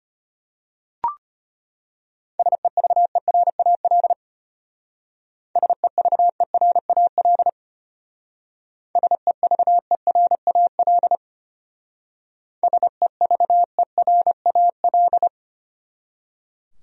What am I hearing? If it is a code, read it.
Morse code, heard as a keyed tone, about 31 wpm, 700 Hz. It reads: SEVERAL SEVERAL SEVERAL SEVERAL
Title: Speed-Racing Top 500 Words 25wpm